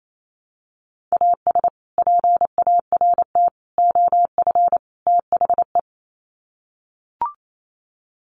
Morse code, heard as a keyed tone, about 28 wpm, 700 Hz. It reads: AS PART OF THE
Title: Sets of 4 Words 28wpm